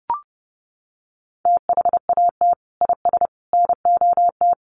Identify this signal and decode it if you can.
Morse code, heard as a keyed tone, about 30 wpm, 700 Hz. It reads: THAT IS NOT
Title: Sentences from Top 1300 Words 30wpm